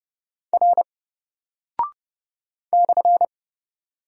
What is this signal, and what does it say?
Morse code, heard as a keyed tone, about 31 wpm, 700 Hz. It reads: R /
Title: Single Character (Letter:Number:Punctuation:Prosign) 30wpm